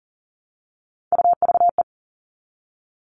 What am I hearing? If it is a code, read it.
Morse code, heard as a keyed tone, about 40 wpm, 700 Hz. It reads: UVE
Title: All Cut Numbers - 3-Digit 40wpm